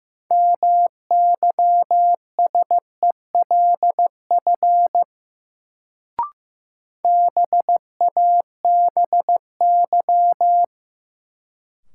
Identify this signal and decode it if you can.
Morse code, heard as a keyed tone, about 15 wpm, 700 Hz. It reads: MYSELF BABY
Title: Sentences from Top 1300 Words - New Word Drill - 15wpm